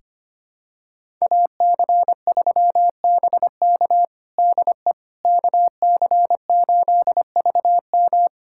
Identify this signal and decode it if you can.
Morse code, heard as a keyed tone, about 25 wpm, 700 Hz. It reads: AC3BK DE KC8VM
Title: USA Callsign DE Callsign - 5 Characters - 25wpm